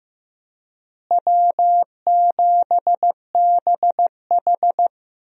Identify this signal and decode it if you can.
Morse code, heard as a keyed tone, about 15 wpm, 700 Hz. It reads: W7BH
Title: USA Call Signs - 4 Characters - Repeated 2x - 15wpm